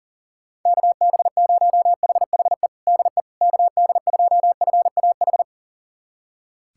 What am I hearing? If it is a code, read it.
Morse code, heard as a keyed tone, about 40 wpm, 700 Hz. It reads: KB0HHE DE KD2FAH
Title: USA Callsign DE Callsign 40wpm